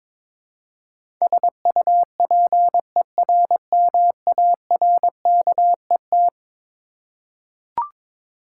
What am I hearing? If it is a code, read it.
Morse code, heard as a keyed tone, about 21 wpm, 700 Hz. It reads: SUPERMARKET
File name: practice / 22wpm / Eleven Letter Words - 22wpm